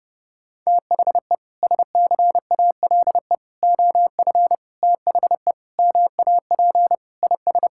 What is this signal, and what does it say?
Morse code, heard as a keyed tone, about 30 wpm, 700 Hz. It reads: THE SCALE OF THE MAP IS
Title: Sentences from Top 1000 Words - Review Part 5 - 30wpm